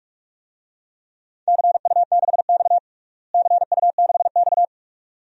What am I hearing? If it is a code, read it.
Morse code, heard as a keyed tone, about 47 wpm, 700 Hz. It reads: CU6X CU6X
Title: DXCC-100 Call Signs - 4 Characters - Repeated 2x - 45wpm